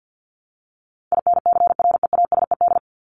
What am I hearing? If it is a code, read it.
Morse code, heard as a keyed tone, about 50 wpm, 700 Hz. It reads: INCREASED
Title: ED Suffix - Top 100 Words with Base Words 50wpm